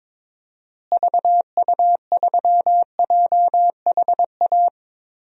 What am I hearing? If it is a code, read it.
Morse code, heard as a keyed tone, about 22 wpm, 700 Hz. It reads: VU3JHA
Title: DXCC-100 Call Signs - 6 Characters - Repeated 2x - 22wpm